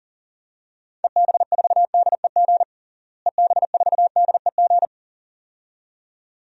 Morse code, heard as a keyed tone, about 37 wpm, 700 Hz. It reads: EB4DEG EB4DEG
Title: DXCC-100 Call Signs - 6 Characters - Repeated 2x - 40wpm